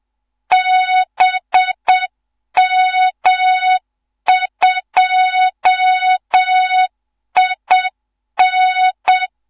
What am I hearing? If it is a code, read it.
Morse code, heard as a keyed tone, about 7 wpm, 3700 Hz, with noise. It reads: BM2IN
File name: Morze-HA2MN / hang / 15 GYAKORLAT